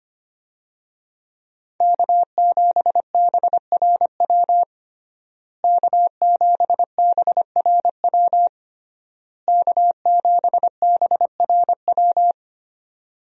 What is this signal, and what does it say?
Morse code, heard as a keyed tone, about 25 wpm, 700 Hz. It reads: K7BRW K7BRW K7BRW
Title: USA Call Signs - 5 Characters - Repeated 3x - 25wpm